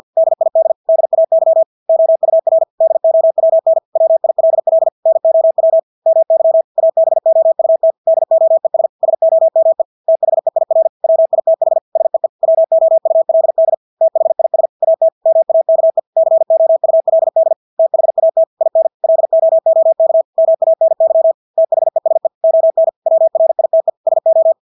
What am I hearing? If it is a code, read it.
Morse code, heard as a keyed tone, about 50 wpm, 650 Hz. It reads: DID DAY OUR DOWN WILL NOW MY ABOUT DOES SOME THEIR WITH SEE WOULD THIS AT MAKE COULD THAT IN LOOK MANY THESE ON WRITE SO
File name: Engelske ord 50 wpm